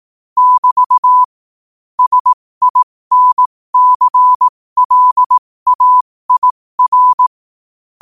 Morse code, heard as a keyed tone, about 18 wpm, spaced as for 15 wpm, 1000 Hz. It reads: <BT> SINCLAIR
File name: qotdfiles / 15wpm